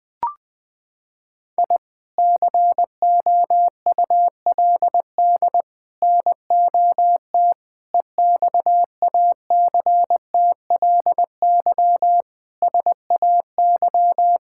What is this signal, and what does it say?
Morse code, heard as a keyed tone, about 20 wpm, 700 Hz. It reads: I COULD NOT EXACTLY SAY
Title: Sentences from Top 1000 Words - Review Part 6 - 20wpm